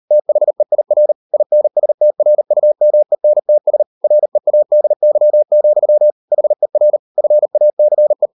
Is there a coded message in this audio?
Yes: Morse code from 0.1 to 8.3 s, heard as a keyed tone, about 40 wpm, 600 Hz. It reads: THEIR INSTRUMENTS READY, HER FACE